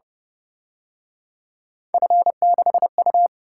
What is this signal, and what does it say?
Morse code, heard as a keyed tone, about 30 wpm, 700 Hz. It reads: F6U
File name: Speed-Racing - Three Letter-Number Combinations - 25wpm